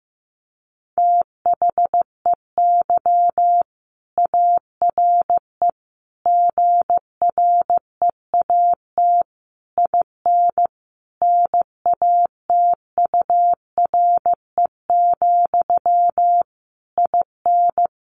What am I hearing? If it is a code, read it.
Morse code, heard as a keyed tone, about 15 wpm, 700 Hz. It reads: THEY ARE GREAT IN NATURE, IN
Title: Sentences from Top 1400 Words 15wpm